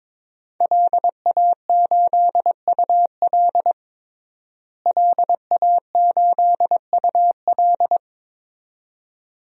Morse code, heard as a keyed tone, about 22 wpm, 700 Hz. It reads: LA8UL LA8UL
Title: DXCC-100 Call Signs - 5 Characters - Repeated 2x - 22wpm